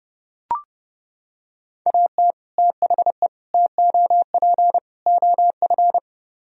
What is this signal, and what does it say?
Morse code, heard as a keyed tone, about 31 wpm, 700 Hz. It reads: AT THE TOP OF